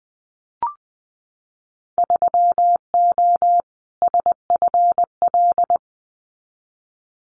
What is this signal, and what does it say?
Morse code, heard as a keyed tone, about 20 wpm, 700 Hz. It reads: 3O SFL
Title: Winter Field Day Exchanges 20wpm